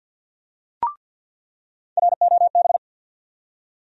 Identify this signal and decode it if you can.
Morse code, heard as a keyed tone, about 47 wpm, 700 Hz. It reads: ROB
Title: USA Call Sign Postfixes - 3 Letters 50wpm